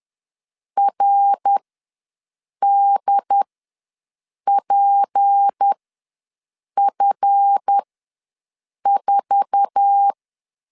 Morse code, heard as a keyed tone, about 10 wpm, spaced as for 6 wpm, 800 Hz. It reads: RDPF4
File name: LEZIONI / 8 YJ45 / 8 YJ4 ws 10wpm 4020